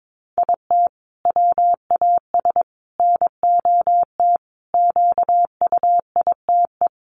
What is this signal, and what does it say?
Morse code, heard as a keyed tone, about 22 wpm, 700 Hz. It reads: IT WAS NOT QUITE
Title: Sentences from Top 700 Words 22wpm